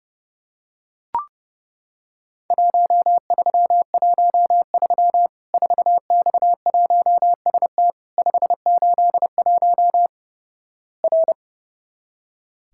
Morse code, heard as a keyed tone, about 32 wpm, 700 Hz. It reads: 1313 4X1ST 581 R
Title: Worked All Europe DX Contest - QTC Exchange - 30wpm